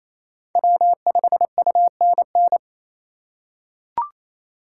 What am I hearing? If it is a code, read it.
Morse code, heard as a keyed tone, about 28 wpm, 700 Hz. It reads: W5UNN
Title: USA Call Signs - 5 Characters - 6x Character Spacing - 28wpm